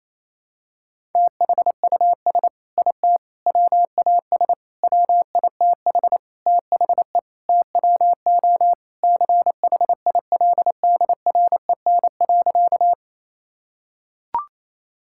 Morse code, heard as a keyed tone, about 28 wpm, 700 Hz. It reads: THUS IT WAS WITH THE TWO CHILDREN.
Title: Sentences from Top 1000 Words - Review Part 5 - 28wpm